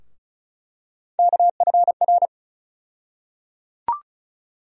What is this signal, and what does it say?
Morse code, heard as a keyed tone, about 36 wpm, 700 Hz. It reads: KFR
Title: USA Call Sign Postfixes - 3 Letters 35wpm